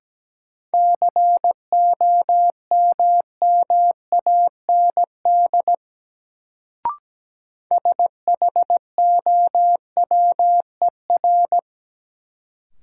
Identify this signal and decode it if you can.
Morse code, heard as a keyed tone, about 17 wpm, 700 Hz. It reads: COMMAND SHOWER
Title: Sentences from Top 1600 Words - New Word Drill - 17wpm